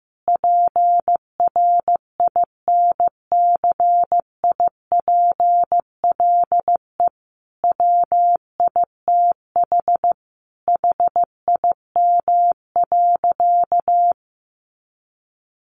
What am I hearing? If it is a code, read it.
Morse code, heard as a keyed tone, about 15 wpm, 700 Hz. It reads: PRINCIPLE WITH HIM.
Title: Sentences from Top 1400 Words 15wpm